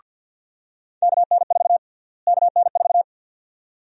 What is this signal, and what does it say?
Morse code, heard as a keyed tone, about 52 wpm, 700 Hz. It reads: KN4 KN4
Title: USA Call Sign Prefixes - Repeated 2x - 50wpm